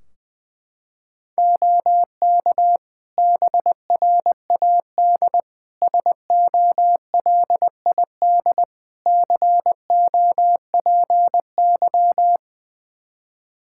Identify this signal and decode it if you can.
Morse code, heard as a keyed tone, about 20 wpm, 700 Hz. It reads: OK BRAD SOLID COPY
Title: Practice QSO - Exchange 3 - More Info - 20wpm